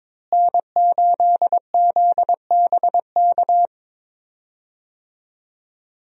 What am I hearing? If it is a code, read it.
Morse code, heard as a keyed tone, about 21 wpm, 700 Hz. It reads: N8ZBK
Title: USA Call Signs - 5 Characters - Repeated 3x - 22wpm